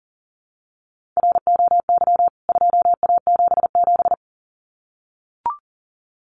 Morse code, heard as a keyed tone, about 40 wpm, 700 Hz. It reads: ROY 2A77
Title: CWT Exchange 40wpm